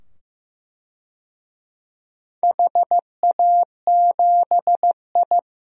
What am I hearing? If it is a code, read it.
Morse code, heard as a keyed tone, about 15 wpm, 700 Hz. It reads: HA7I
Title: DXCC-100 Call Signs - 4 Characters - 15wpm